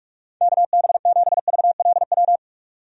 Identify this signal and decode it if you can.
Morse code, heard as a keyed tone, about 46 wpm, 700 Hz. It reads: KB7VLW
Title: USA Call Signs - 6 Characters - 45wpm